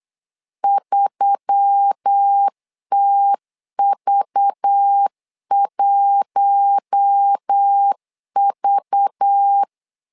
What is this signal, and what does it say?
Morse code, heard as a keyed tone, about 8 wpm, 800 Hz. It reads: 3TV1V
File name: LEZIONI / 5 LBVC / 5 LBV 8wpm 2837